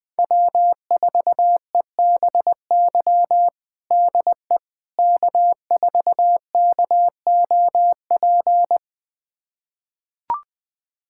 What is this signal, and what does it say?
Morse code, heard as a keyed tone, about 20 wpm, 700 Hz. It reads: W4EBY DE K4KOP
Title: USA Callsign DE Callsign - 5 Characters - 20wpm